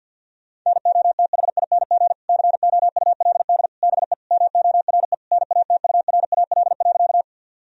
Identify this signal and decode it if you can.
Morse code, heard as a keyed tone, about 50 wpm, 700 Hz. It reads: NOTHING COULD BE MORE NATURAL.
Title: Sentences from Top 1000 Words - Review Part 4 - 50wpm